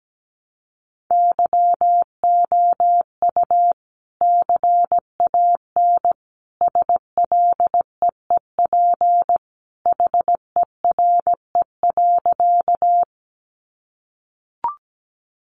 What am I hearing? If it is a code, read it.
Morse code, heard as a keyed tone, about 17 wpm, 700 Hz. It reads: YOU CAN SLEEP HERE.
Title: Sentences from Top 500 Words 17wpm